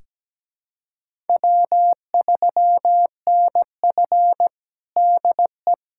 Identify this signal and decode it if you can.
Morse code, heard as a keyed tone, about 17 wpm, 700 Hz. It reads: W3NF DE
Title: USA Callsign DE Callsign - 4 Characters - 17wpm